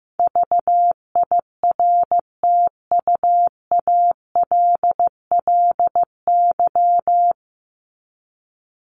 Morse code, heard as a keyed tone, about 15 wpm, 700 Hz. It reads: VIRTUALLY
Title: Top 1501-1750 Words 15wpm